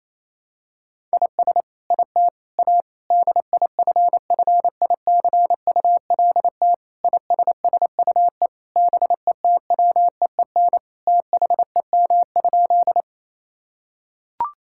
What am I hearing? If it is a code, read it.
Morse code, heard as a keyed tone, about 27 wpm, 700 Hz. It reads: IS IT A DIFFICULT ISSUE BETWEEN THEM?
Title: Sentences from Top 1000 Words - Review Part 6 - 28wpm